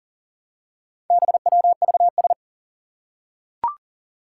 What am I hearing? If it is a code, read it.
Morse code, heard as a keyed tone, about 37 wpm, 700 Hz. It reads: BWVS
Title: Four Challenging Letter-Number Combinations 40wpm